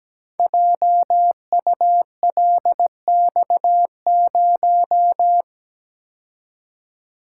Morse code, heard as a keyed tone, about 17 wpm, 700 Hz. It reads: JULX0
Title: Code Groups - 5 Letter:Numbers - 17wpm